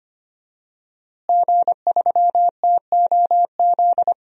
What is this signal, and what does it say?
Morse code, heard as a keyed tone, about 25 wpm, 700 Hz. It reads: G3TOZ
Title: DXCC-100 Call Signs - 5 Characters - Repeated 3x - 25wpm